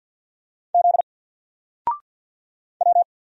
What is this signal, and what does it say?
Morse code, heard as a keyed tone, about 48 wpm, 700 Hz. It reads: Z W
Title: Single Character (Letter:Number:Punctuation:Prosign) Alt 50wpm